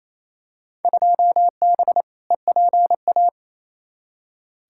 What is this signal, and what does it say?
Morse code, heard as a keyed tone, about 28 wpm, 700 Hz. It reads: 2B EPA
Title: Field Day Exchanges Encore - 28wpm